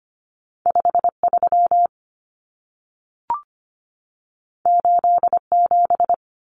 Morse code, heard as a keyed tone, about 25 wpm, 700 Hz. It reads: 53 87
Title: Common Cut Numbers - 2-Digit 25wpm